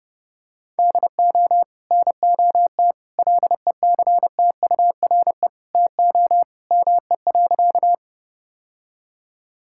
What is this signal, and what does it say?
Morse code, heard as a keyed tone, about 30 wpm, 700 Hz. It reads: DO NOT LECTURE TO ME.
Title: Sentences from Top 1600 Words 30wpm